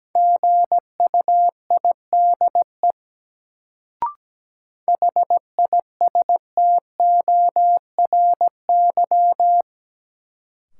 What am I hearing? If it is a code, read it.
Morse code, heard as a keyed tone, about 17 wpm, 700 Hz. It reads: GUIDE HISTORY
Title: Sentences from Top 600 Words - New Word Drill - 17wpm